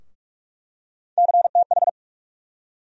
Morse code, heard as a keyed tone, about 41 wpm, 700 Hz. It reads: CTH
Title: USA Call Sign Postfixes - 3 Letters 45wpm